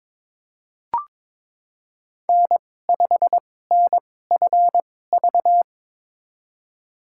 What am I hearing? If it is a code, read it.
Morse code, heard as a keyed tone, about 22 wpm, spaced as for 16 wpm, 690 Hz. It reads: N5NFV
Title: USA Call Signs - 5 Characters - 2x Character Spacing - 22wpm